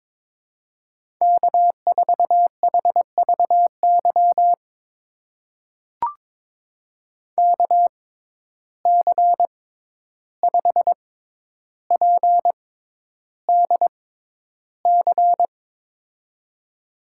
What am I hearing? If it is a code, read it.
Morse code, heard as a keyed tone, about 22 wpm, spaced as for 11 wpm, 700 Hz. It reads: K4HVY KC5PDC
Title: USA Call Signs - 6x Character Spacing - 22wpm